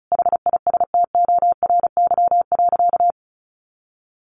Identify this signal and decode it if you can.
Morse code, heard as a keyed tone, about 34 wpm, 700 Hz. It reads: HISTORY.
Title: A Sound Of Thunder 35wpm